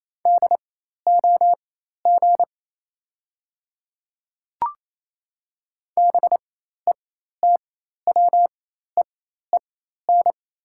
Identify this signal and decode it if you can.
Morse code, heard as a keyed tone, about 28 wpm, spaced as for 13 wpm, 700 Hz. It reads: DOG BETWEEN
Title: 4x Character Spacing - Top 500 Words - 28wpm